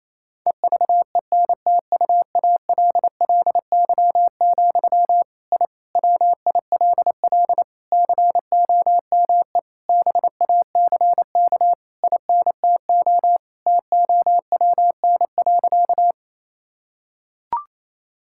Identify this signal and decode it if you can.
Morse code, heard as a keyed tone, about 28 wpm, 700 Hz. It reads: EVENTUALLY, I WILL COME BACK INTO TOWN.
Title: Sentences from Top 1000 Words - Review Part 6 - 28wpm